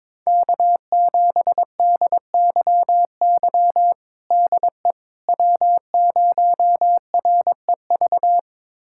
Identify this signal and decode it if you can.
Morse code, heard as a keyed tone, about 22 wpm, 700 Hz. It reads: K7DYY DE W0REV
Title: USA Callsign DE Callsign - 5 Characters - 22wpm